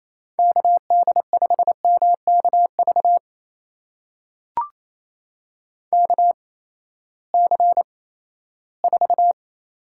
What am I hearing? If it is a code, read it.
Morse code, heard as a keyed tone, about 28 wpm, spaced as for 16 wpm, 700 Hz. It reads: KD5MKV KC4
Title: USA Call Signs - 6 Characters - 8x Character Spacing - 28wpm